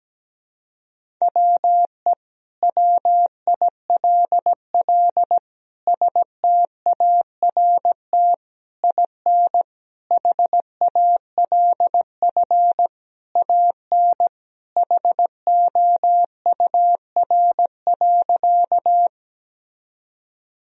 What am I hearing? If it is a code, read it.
Morse code, heard as a keyed tone, about 17 wpm, 700 Hz. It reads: WE WILL START IN HALF AN HOUR.
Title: Sentences from Top 1000 Words - Review Part 1 - 17wpm